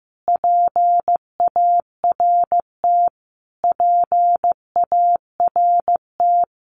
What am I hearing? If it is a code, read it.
Morse code, heard as a keyed tone, about 15 wpm, 700 Hz. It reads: PART PART
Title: Top 200 Words Repeated 15wpm